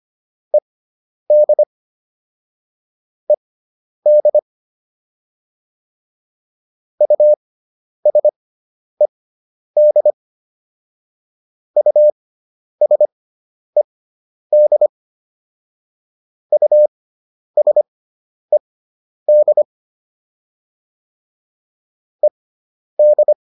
Morse code, heard as a keyed tone, about 25 wpm, spaced as for 10 wpm, 600 Hz. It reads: ED ED USED USED USED ED